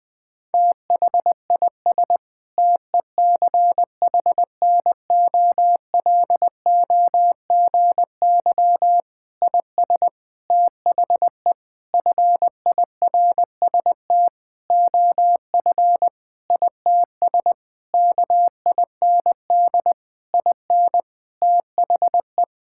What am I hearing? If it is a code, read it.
Morse code, heard as a keyed tone, about 20 wpm, 700 Hz. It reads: THIS TECHNOLOGY IS THE FIRST OF ITS KIND IN THE
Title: Sentences with 8-Word Phrases - 20wpm